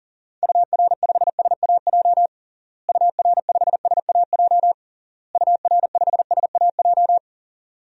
Morse code, heard as a keyed tone, about 37 wpm, 700 Hz. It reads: UR5SAJ UR5SAJ UR5SAJ
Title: DXCC-100 Call Signs - 6 Characters - Repeated 3x - 40wpm